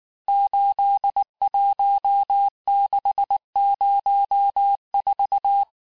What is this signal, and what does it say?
Morse code, heard as a keyed tone, about 19 wpm, 800 Hz. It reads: 81604
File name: cours-du-lundi files / cours-1589-texte-chiffre-vitesse-18mpm